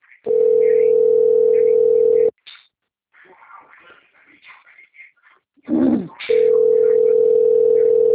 Ringback tone (North America), 440 + 480 Hz.